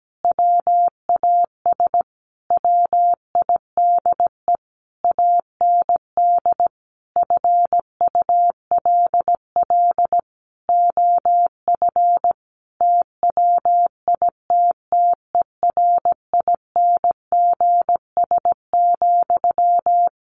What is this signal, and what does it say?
Morse code, heard as a keyed tone, about 17 wpm, 700 Hz. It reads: WAS WIDE AND FULL OF TWITTERINGS,
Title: A Sound Of Thunder 17wpm